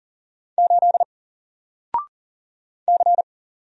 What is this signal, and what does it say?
Morse code, heard as a keyed tone, about 40 wpm, 700 Hz. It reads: 8 C
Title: Single Letter-Number 40wpm